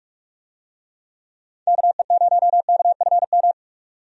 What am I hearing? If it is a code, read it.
Morse code, heard as a keyed tone, about 42 wpm, 700 Hz. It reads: KE0KFM
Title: USA Call Signs - 6 Characters - 45wpm